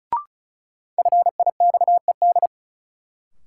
Morse code, heard as a keyed tone, about 34 wpm, 700 Hz. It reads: FIXED
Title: Sentences from Top 1100 Words - New Word Drill Alt - 35wpm